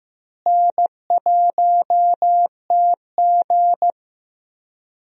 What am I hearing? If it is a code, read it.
Morse code, heard as a keyed tone, about 15 wpm, 700 Hz. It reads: N1TG
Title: USA Call Signs - 4 Characters - Repeated 3x - 15wpm